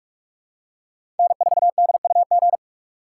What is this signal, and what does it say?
Morse code, heard as a keyed tone, about 46 wpm, 700 Hz. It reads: N4DUG
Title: USA Call Signs - 4x Character Spacing - 45wpm